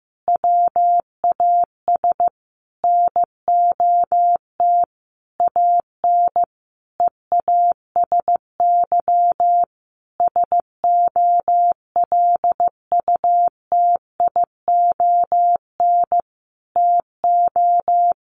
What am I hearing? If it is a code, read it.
Morse code, heard as a keyed tone, about 15 wpm, 700 Hz. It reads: WAS NOT AN EASY SOLUTION TO
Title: Sentences from Top 1300 Words 15wpm